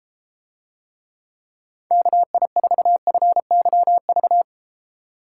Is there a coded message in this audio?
Yes: Morse code from 1.9 to 4.4 s, heard as a keyed tone, about 32 wpm, 700 Hz. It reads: KI4FYV